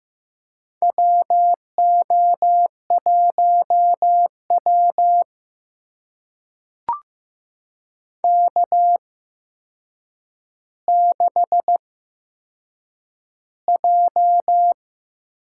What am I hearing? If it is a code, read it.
Morse code, heard as a keyed tone, about 15 wpm, spaced as for 7 wpm, 700 Hz. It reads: WO1W K6J
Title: USA Call Signs - 4 Characters - 8x Character Spacing - 15wpm